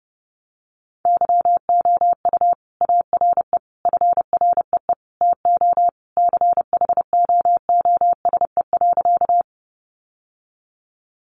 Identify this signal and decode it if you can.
Morse code, heard as a keyed tone, about 30 wpm, 700 Hz. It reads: YOU ARE FREE TO CHOOSE.
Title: Sentences from Top 700 Words 30wpm